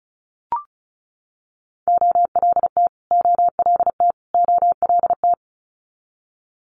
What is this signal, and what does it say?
Morse code, heard as a keyed tone, about 36 wpm, 700 Hz. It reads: OLT OLT OLT
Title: Multi-letter Phonemes Repeated 3x 35wpm